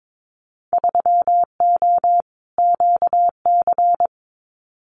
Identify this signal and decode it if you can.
Morse code, heard as a keyed tone, about 22 wpm, 700 Hz. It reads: 3O QC
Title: Winter Field Day Exchanges Encore 22wpm